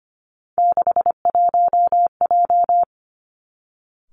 Morse code, heard as a keyed tone, about 25 wpm, 700 Hz. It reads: 61J
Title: Three Challenging Letter-Number Combinations 25wpm